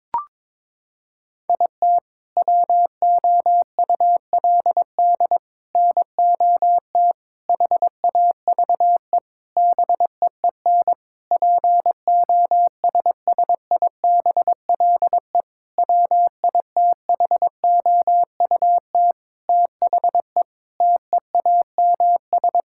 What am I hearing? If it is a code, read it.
Morse code, heard as a keyed tone, about 21 wpm, 700 Hz. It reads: IT WOULD NOT HAVE BEEN POSSIBLE WITHOUT THE TEAMS
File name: Sentences with 7-Word Phrases - 22wpm